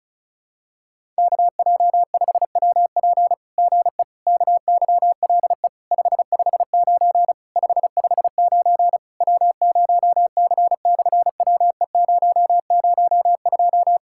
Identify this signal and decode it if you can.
Morse code, heard as a keyed tone, about 36 wpm, 700 Hz. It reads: KJ5WP GE KYLE 559 559 W0C/WE002
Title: SOTA QSO Practice 35wpm